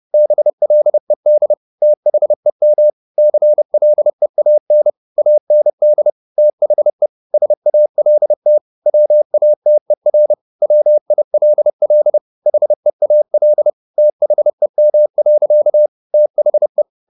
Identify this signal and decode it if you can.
Morse code, heard as a keyed tone, about 30 wpm, 600 Hz. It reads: BLED THEM CLEAN AND THE SALT WATER WILL HEAL THEM. THE